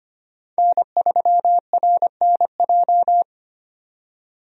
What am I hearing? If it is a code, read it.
Morse code, heard as a keyed tone, about 25 wpm, 690 Hz. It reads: N3RNJ
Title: USA Call Signs - 5 Characters - 25wpm